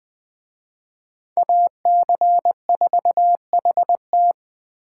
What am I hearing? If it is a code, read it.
Morse code, heard as a keyed tone, about 24 wpm, 700 Hz. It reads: AC4HT
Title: Speed Racing - USA Call Signs 20wpm